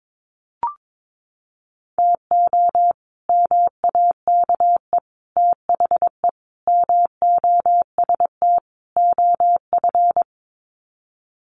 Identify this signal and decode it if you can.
Morse code, heard as a keyed tone, about 22 wpm, 700 Hz. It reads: TO MAKE THE MOST OF
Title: Sets of 5 Words Encore2 22wpm